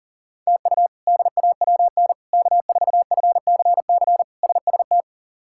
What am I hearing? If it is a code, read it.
Morse code, heard as a keyed tone, about 40 wpm, 700 Hz. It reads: TU DAWN K4FCC SST
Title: K1USN SST QSO Practice - 40wpm